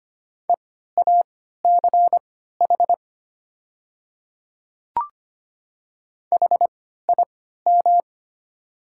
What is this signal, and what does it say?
Morse code, heard as a keyed tone, about 25 wpm, spaced as for 14 wpm, 700 Hz. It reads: EACH HIM
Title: 3x Character Spacing - Top 100 Words - 25wpm